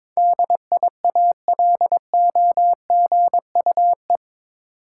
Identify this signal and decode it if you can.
Morse code, heard as a keyed tone, about 22 wpm, 700 Hz. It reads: DIALOGUE
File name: practice / 22wpm / Eight Letter Words - 22wpm